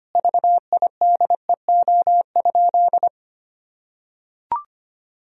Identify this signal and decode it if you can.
Morse code, heard as a keyed tone, about 25 wpm, 700 Hz. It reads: VIDEO?
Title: Sentences from Top 1000 Words - Review Part 4 - 25wpm